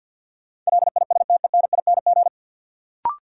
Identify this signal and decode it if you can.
Morse code, heard as a keyed tone, about 53 wpm, 700 Hz. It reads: LISTENING
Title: Sets of 5 Words - New Word Drill - 50wpm